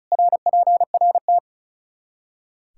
Morse code, heard as a keyed tone, about 36 wpm, 700 Hz. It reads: RPRT
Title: K1USN SST QSO Practice 35wpm